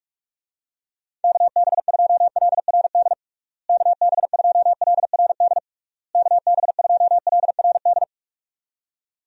Morse code, heard as a keyed tone, about 46 wpm, 700 Hz. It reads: KB2LRD KB2LRD KB2LRD
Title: USA Call Signs - 6 Characters - Repeated 3x - 45wpm